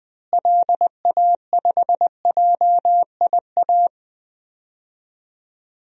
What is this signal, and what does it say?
Morse code, heard as a keyed tone, about 20 wpm, 700 Hz. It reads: LA5JIA
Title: DXCC-100 Call Signs - 6 Characters - Repeated 2x - 20wpm